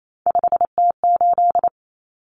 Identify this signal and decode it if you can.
Morse code, heard as a keyed tone, about 28 wpm, 700 Hz. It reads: 5T8
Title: Common Cut Numbers - 3-Digit 28wpm